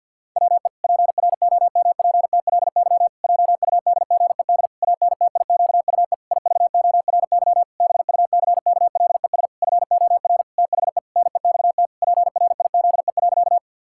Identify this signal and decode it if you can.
Morse code, heard as a keyed tone, about 50 wpm, 700 Hz. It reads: WE PROMPTLY JUDGED ANTIQUE IVORY BUCKLES FOR THE NEXT PRIZE.
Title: Pangram Challenge v2 - 50wpm